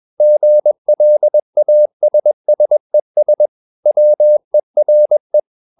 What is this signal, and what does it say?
Morse code, heard as a keyed tone, about 21 wpm, 600 Hz. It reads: GLASSES WERE
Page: CW med Gnister